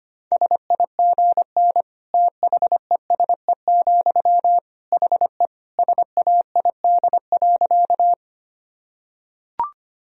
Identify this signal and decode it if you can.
Morse code, heard as a keyed tone, about 25 wpm, 700 Hz. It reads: SIGN THESE, HE SAID.
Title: Sentences from Top 1000 Words - Review Part 1 - 25wpm